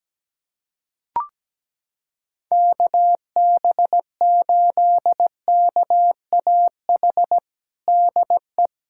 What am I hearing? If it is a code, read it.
Morse code, heard as a keyed tone, about 17 wpm, 700 Hz. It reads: KB8KAH DE
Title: USA Callsign DE Callsign - 6 Characters - 17wpm